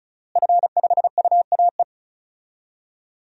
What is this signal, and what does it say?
Morse code, heard as a keyed tone, about 36 wpm, 700 Hz. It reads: F5UAE
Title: DXCC-100 Call Signs - 5 Characters - Repeated 3x - 35wpm